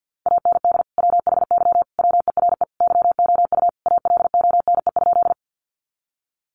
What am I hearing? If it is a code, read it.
Morse code, heard as a keyed tone, about 50 wpm, 700 Hz. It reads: AND WHY WERE YOU ALONE?
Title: Sentences from Top 1000 Words 50wpm